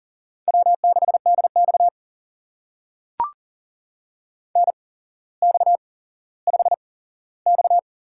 Morse code, heard as a keyed tone, about 42 wpm, spaced as for 17 wpm, 700 Hz. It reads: W6DX NX5X
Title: USA Call Signs - 4 Characters - 8x Character Spacing - 40wpm